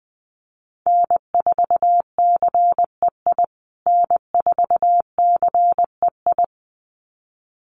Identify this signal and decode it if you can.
Morse code, heard as a keyed tone, about 20 wpm, 700 Hz. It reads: N4CEI N4CEI
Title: USA Call Signs Repeated 2x 20wpm